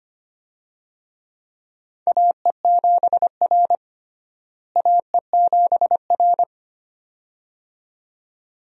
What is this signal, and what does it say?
Morse code, heard as a keyed tone, about 25 wpm, 700 Hz. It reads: AE7R AE7R
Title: USA Call Signs - 4 Characters - Repeated 2x - 25wpm